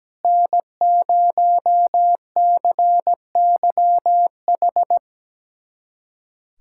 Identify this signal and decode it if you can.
Morse code, heard as a keyed tone, about 17 wpm, 700 Hz. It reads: N0CYH
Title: USA Callsign DE Callsign 17wpm